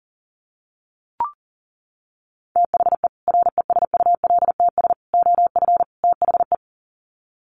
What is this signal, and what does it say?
Morse code, heard as a keyed tone, about 40 wpm, 700 Hz. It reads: THE RESULTS OF THE
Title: Sets of 4 Words 40wpm